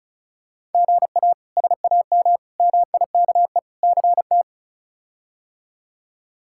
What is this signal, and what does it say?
Morse code, heard as a keyed tone, about 34 wpm, 700 Hz. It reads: GA SAM MIKE CT
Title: K1USN SST QSO Practice - 35wpm